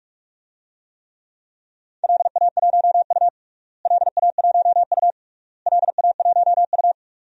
Morse code, heard as a keyed tone, about 46 wpm, 700 Hz. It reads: LA1U LA1U LA1U
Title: DXCC-100 Call Signs - 4 Characters - Repeated 3x - 45wpm